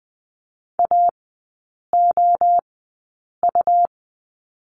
Morse code, heard as a keyed tone, about 20 wpm, spaced as for 8 wpm, 700 Hz. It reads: AOU
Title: 127 Lesson 21e - Number 6 - Three Characters 20wpm